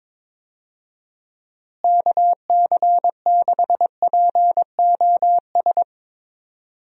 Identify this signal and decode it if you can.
Morse code, heard as a keyed tone, about 21 wpm, 700 Hz. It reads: KC6POS
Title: USA Call Signs - 6 Characters - Repeated 2x - 22wpm